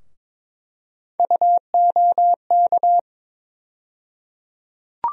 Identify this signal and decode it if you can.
Morse code, heard as a keyed tone, about 21 wpm, 700 Hz. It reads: UOK
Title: USA Call Sign Postfixes - 3 Letters 22wpm